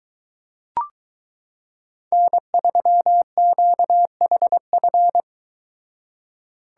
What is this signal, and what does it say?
Morse code, heard as a keyed tone, about 23 wpm, 700 Hz. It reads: N3QHF